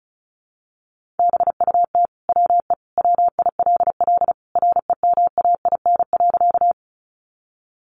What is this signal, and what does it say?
Morse code, heard as a keyed tone, about 34 wpm, 700 Hz. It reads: BUT WE WILL REMAIN.
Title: Sentences from Top 1200 Words 35wpm